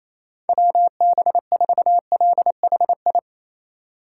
Morse code, heard as a keyed tone, about 28 wpm, 700 Hz. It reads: WB4LHI
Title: USA Call Signs - 6 Characters - 3x Character Spacing - 28wpm